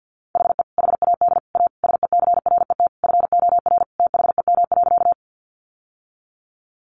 Morse code, heard as a keyed tone, about 49 wpm, 700 Hz. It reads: HE HAD A SECRET FOR THEM.
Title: Sentences from Top 1200 Words 50wpm